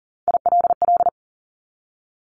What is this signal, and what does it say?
Morse code, heard as a keyed tone, about 40 wpm, 700 Hz. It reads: ILL
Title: Top 1751-2000 Words 40wpm